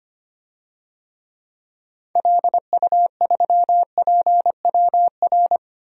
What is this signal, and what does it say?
Morse code, heard as a keyed tone, about 25 wpm, 700 Hz. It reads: LU3PWR
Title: DXCC-100 Call Signs - 6 Characters - 25wpm